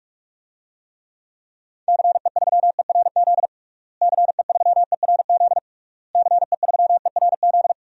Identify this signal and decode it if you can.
Morse code, heard as a keyed tone, about 46 wpm, 700 Hz. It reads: CE3ERZ CE3ERZ CE3ERZ
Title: DXCC-100 Call Signs - 6 Characters - Repeated 3x - 45wpm